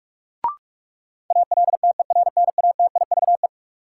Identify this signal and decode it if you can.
Morse code, heard as a keyed tone, about 42 wpm, 700 Hz. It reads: ALTERNATIVE
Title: Sentences from Top 1000 Words - Review Part 4 - New Word Drill - 45wpm